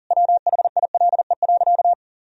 Morse code, heard as a keyed tone, about 40 wpm, 700 Hz. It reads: WHILE.
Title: Sentences from Top 500 Words and Abbreviations 40wpm